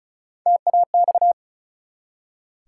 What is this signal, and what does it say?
Morse code, heard as a keyed tone, about 36 wpm, 700 Hz. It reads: TAX
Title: Three Letter Words - 35wpm